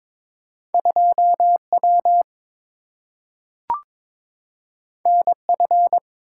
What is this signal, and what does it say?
Morse code, heard as a keyed tone, about 22 wpm, 700 Hz. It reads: 2W NF
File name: practice / 22wpm / Two Letter-Number Combinations - 22wpm